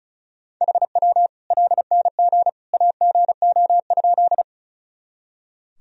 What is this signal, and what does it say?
Morse code, heard as a keyed tone, about 36 wpm, 700 Hz. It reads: HW LNG AGO?
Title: Sentences from Top 500 Words and Abbreviations 35wpm